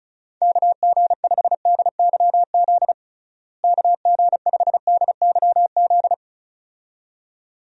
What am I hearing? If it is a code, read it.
Morse code, heard as a keyed tone, about 34 wpm, 700 Hz. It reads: KG5DYZ KG5DYZ
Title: USA Call Signs - 6 Characters - Repeated 2x - 35wpm